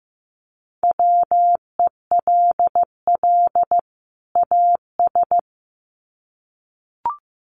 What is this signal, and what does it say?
Morse code, heard as a keyed tone, about 15 wpm, 700 Hz. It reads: WELL AS
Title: Sets of 2 Words 15wpm